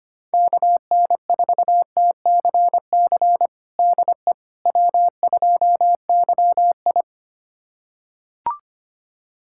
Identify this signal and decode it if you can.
Morse code, heard as a keyed tone, about 25 wpm, 700 Hz. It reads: KN4TCC DE W2YI
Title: USA Callsign DE Callsign 25wpm